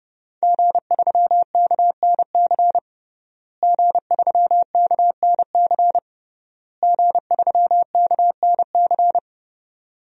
Morse code, heard as a keyed tone, about 30 wpm, 700 Hz. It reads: G3KNC G3KNC G3KNC
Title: DXCC-100 Call Signs - 5 Characters - Repeated 3x - 30wpm